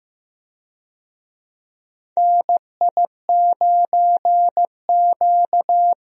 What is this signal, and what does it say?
Morse code, heard as a keyed tone, about 15 wpm, 700 Hz. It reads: NI9Q
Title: USA Call Signs - 4 Characters - Repeated 2x - 15wpm